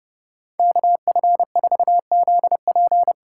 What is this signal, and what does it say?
Morse code, heard as a keyed tone, about 30 wpm, 700 Hz. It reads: KF4ZP
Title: USA Call Signs - 6 Characters - 6x Character Spacing - 30wpm